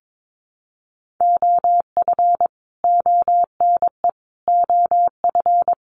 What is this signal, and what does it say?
Morse code, heard as a keyed tone, about 22 wpm, 700 Hz. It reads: OF ONE OF
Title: Sets of 4 Words 22wpm